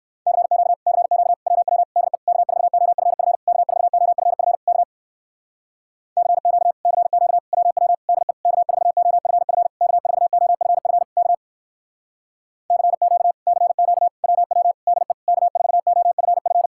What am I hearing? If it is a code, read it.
Morse code, heard as a keyed tone, about 58 wpm, 700 Hz. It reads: CQ CQ WW DE K4OFF K4OFF K CQ CQ WW DE K4OFF K4OFF K CQ CQ WW DE K4OFF